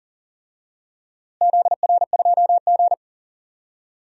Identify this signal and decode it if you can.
Morse code, heard as a keyed tone, about 39 wpm, 700 Hz. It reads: ZR2G
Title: DXCC-100 Call Signs - 4 Characters - Repeated 3x - 40wpm